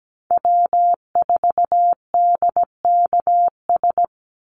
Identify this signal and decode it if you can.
Morse code, heard as a keyed tone, about 17 wpm, 700 Hz. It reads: W4DKS
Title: Call Signs Repeated 3x - 17wpm